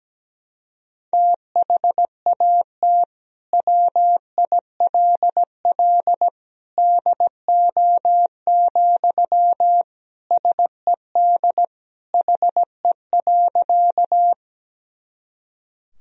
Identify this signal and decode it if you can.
Morse code, heard as a keyed tone, about 17 wpm, 700 Hz. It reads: THAT WILL DO, SED HE.
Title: Sentences from Top 100 Words and Abbreviations 17wpm